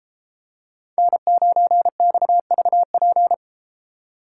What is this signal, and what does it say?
Morse code, heard as a keyed tone, about 32 wpm, 700 Hz. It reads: N9XVP